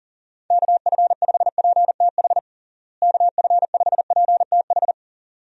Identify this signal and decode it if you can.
Morse code, heard as a keyed tone, about 40 wpm, 700 Hz. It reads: KF5PTH KF5PTH
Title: USA Call Signs - 6 Characters - Repeated 2x - 40wpm